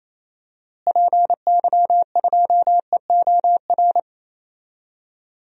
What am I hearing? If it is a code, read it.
Morse code, heard as a keyed tone, about 28 wpm, 700 Hz. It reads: PY2EOR
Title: DXCC-100 Call Signs - 6 Characters - Repeated 2x - 28wpm